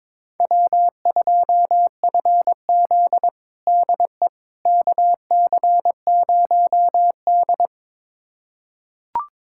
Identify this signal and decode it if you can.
Morse code, heard as a keyed tone, about 22 wpm, 700 Hz. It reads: W2FZ DE KC0D
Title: USA Callsign DE Callsign 22wpm